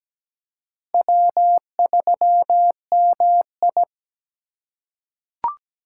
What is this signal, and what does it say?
Morse code, heard as a keyed tone, about 17 wpm, 700 Hz. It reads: W3MI
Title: USA Call Signs - 4 Characters - 3x Character Spacing - 17wpm